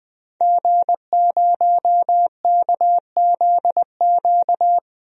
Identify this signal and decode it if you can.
Morse code, heard as a keyed tone, about 21 wpm, 700 Hz. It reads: G0KZQ
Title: DXCC-100 Call Signs - 5 Characters - 20wpm